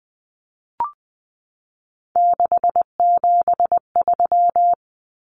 Morse code, heard as a keyed tone, about 20 wpm, 700 Hz. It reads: 673
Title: Common Cut Numbers - 3-Digit 20wpm